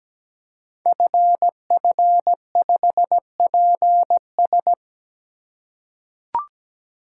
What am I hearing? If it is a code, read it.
Morse code, heard as a keyed tone, about 17 wpm, 700 Hz. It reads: FF5PS
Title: Code Groups - 5 Letter:Numbers - 17wpm